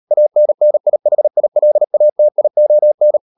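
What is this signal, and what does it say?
Morse code, heard as a keyed tone, about 38 wpm, 600 Hz. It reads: ANNIHILATION
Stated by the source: Morse code practice at 35–39 wpm sending English text of Ken Follett's novel Never, from CW med Gnister.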